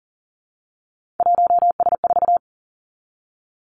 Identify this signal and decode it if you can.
Morse code, heard as a keyed tone, about 41 wpm, 700 Hz. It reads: 1S4
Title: Three Challenging Letter-Number Combinations 40wpm